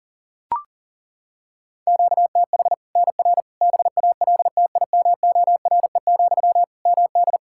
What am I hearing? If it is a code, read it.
Morse code, heard as a keyed tone, about 40 wpm, 700 Hz. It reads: QTH NR BALTIMORE, MD
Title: Practice QSO - Exchange 1 - Beginning - 40wpm